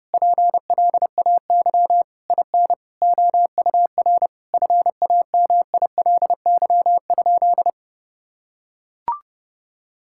Morse code, heard as a keyed tone, about 30 wpm, 700 Hz. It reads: PLAY IN OUR FAMILY?
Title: Sentences from Top 1000 Words - Review Part 6 - 30wpm